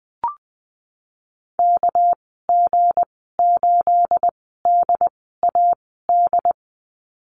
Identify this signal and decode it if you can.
Morse code, heard as a keyed tone, about 20 wpm, spaced as for 15 wpm, 700 Hz. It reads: KG8DAD
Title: 219 Lesson 35g - Number 0 - Call Signs 20wpm